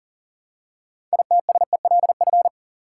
Morse code, heard as a keyed tone, about 40 wpm, 700 Hz. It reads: ITSELF
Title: Six Letter Words - 40wpm